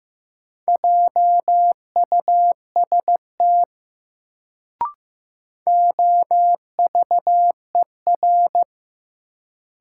Morse code, heard as a keyed tone, about 15 wpm, 700 Hz. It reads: JUST OVER
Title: Sets of 2 Words - New Word Drill - 15wpm